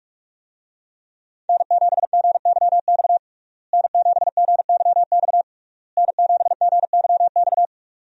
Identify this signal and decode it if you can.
Morse code, heard as a keyed tone, about 46 wpm, 700 Hz. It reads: N7GYX N7GYX N7GYX
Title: USA Call Signs - 5 Characters - Repeated 3x - 45wpm